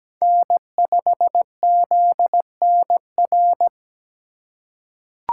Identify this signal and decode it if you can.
Morse code, heard as a keyed tone, about 17 wpm, 700 Hz. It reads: N5ZNR
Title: USA Call Signs - 5 Characters - 4x Character Spacing - 17wpm